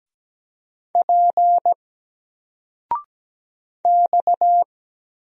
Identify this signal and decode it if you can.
Morse code, heard as a keyed tone, about 17 wpm, 700 Hz. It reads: P X
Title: Single Letter-Number 17wpm